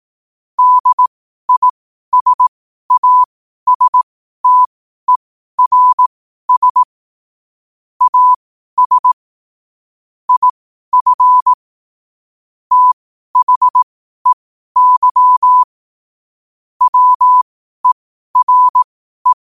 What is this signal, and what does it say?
Morse code, heard as a keyed tone, about 18 wpm, spaced as for 12 wpm, 1000 Hz. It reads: DISASTERS AS IF THEY WERE
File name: qotdfiles / 12wpm